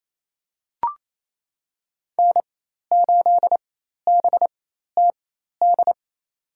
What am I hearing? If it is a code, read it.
Morse code, heard as a keyed tone, about 28 wpm, spaced as for 13 wpm, 700 Hz. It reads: N8BTD
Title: USA Call Signs - 5 Characters - 4x Character Spacing - 28wpm